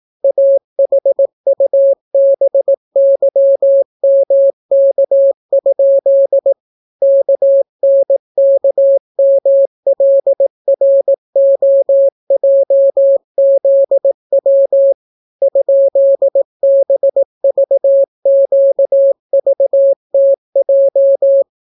Morse code, heard as a keyed tone, about 18 wpm, 550 Hz. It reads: AHUBYMK? KNKMLROJZW ?BVQVTJ